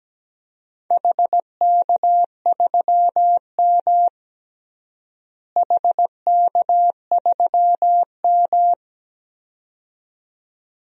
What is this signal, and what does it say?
Morse code, heard as a keyed tone, about 17 wpm, 700 Hz. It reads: HK3M HK3M
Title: DXCC-100 Call Signs - 4 Characters - Repeated 2x - 17wpm